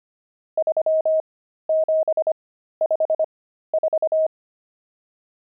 Morse code, heard as a keyed tone, about 25 wpm, spaced as for 13 wpm, 630 Hz. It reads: 3754
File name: pangram-fs-13-cs-25-nums6